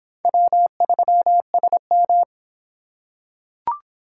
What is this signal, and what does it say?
Morse code, heard as a keyed tone, about 25 wpm, 700 Hz. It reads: W3SM